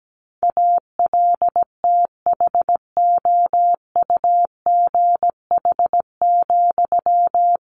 Morse code, heard as a keyed tone, about 17 wpm, 700 Hz. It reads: ALTHOUGH,
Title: Sentences from Top 600 Words 17wpm